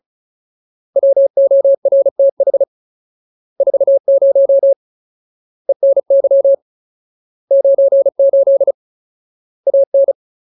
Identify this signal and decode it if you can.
Morse code, heard as a keyed tone, about 34 wpm, 550 Hz. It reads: WORTH 40 ENY 98 AN
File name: Ultimate Word List / 35WPM Ultimate